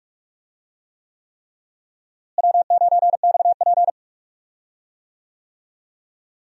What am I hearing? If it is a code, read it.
Morse code, heard as a keyed tone, about 41 wpm, 700 Hz. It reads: W9XP
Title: USA Call Signs - 4 Characters - Repeated 3x - 45wpm